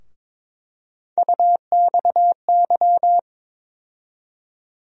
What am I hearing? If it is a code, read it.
Morse code, heard as a keyed tone, about 22 wpm, 700 Hz. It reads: UXY
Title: USA Call Sign Postfixes - 3 Letters 22wpm